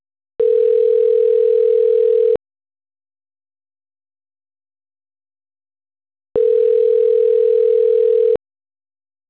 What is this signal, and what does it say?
Ringback tone (North America), 440 + 480 Hz.